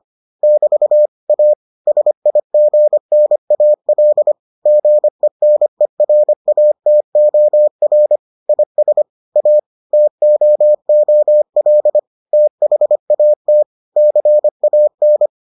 Morse code, heard as a keyed tone, about 25 wpm, 600 Hz. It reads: <BT> A SIGNAL GENERATOR IS A TOOL THAT CAN